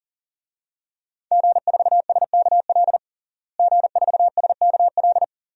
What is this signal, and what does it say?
Morse code, heard as a keyed tone, about 37 wpm, 700 Hz. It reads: G4SKL G4SKL
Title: DXCC-100 Call Signs - 5 Characters - Repeated 3x - 40wpm